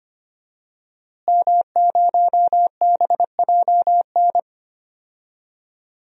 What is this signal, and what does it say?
Morse code, heard as a keyed tone, about 25 wpm, 700 Hz. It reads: M0BJN
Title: DXCC-100 Call Signs - 5 Characters - Repeated 3x - 25wpm